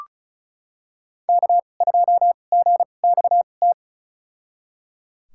Morse code, heard as a keyed tone, about 36 wpm, spaced as for 25 wpm, 700 Hz. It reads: K2GXT
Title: USA Call Signs - 5 Characters - 2x Character Spacing - 35wpm